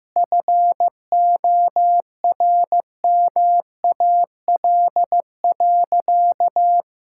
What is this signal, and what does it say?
Morse code, heard as a keyed tone, about 15 wpm, 700 Hz. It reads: FORMAL.
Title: Sentences from Top 1600 Words 15wpm